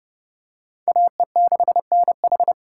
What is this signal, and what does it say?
Morse code, heard as a keyed tone, about 30 wpm, 700 Hz. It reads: AE6NH